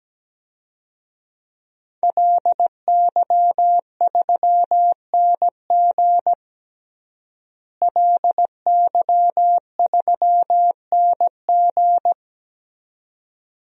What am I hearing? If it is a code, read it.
Morse code, heard as a keyed tone, about 17 wpm, 700 Hz. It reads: LY3NG LY3NG
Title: DXCC-100 Call Signs - 5 Characters - Repeated 2x - 17wpm